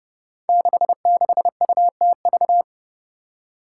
Morse code, heard as a keyed tone, about 29 wpm, 700 Hz. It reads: 66UTV
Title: Five Challenging Letter-Number Combinations 30wpm